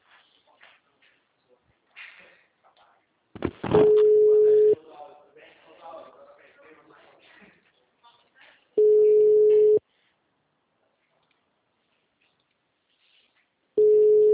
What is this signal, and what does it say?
Ringback tone (Europe, 425 Hz), with speech.
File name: out- -2017- - - .73791